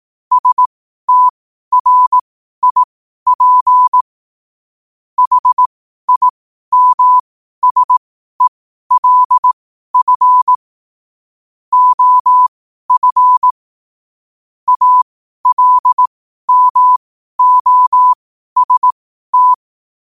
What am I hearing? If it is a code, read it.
Morse code, heard as a keyed tone, about 18 wpm, spaced as for 12 wpm, 1000 Hz. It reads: STRIP HIMSELF OF ALMOST
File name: qotdfiles / 12wpm